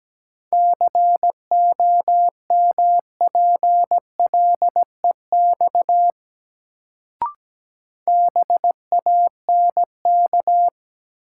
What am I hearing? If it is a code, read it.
Morse code, heard as a keyed tone, about 17 wpm, 700 Hz. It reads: COMPLEX BANK
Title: Sentences from Top 900 Words - New Word Drill - 17wpm